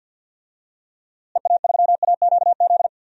Morse code, heard as a keyed tone, about 50 wpm, 700 Hz. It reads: EA3AQZ
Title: DXCC-100 Call Signs - 6 Characters - Repeated 2x - 50wpm